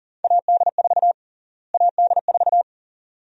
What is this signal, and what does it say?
Morse code, heard as a keyed tone, about 37 wpm, 700 Hz. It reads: AD4 AD4
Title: USA Call Sign Prefixes - Repeated 2x - 40wpm